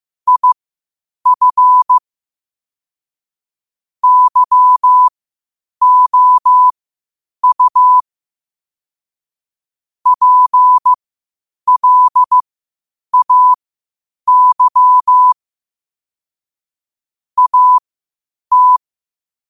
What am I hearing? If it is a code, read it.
Morse code, heard as a keyed tone, about 15 wpm, spaced as for 8 wpm, 1000 Hz. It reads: IF YOU PLAY AT
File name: qotdfiles / 8wpm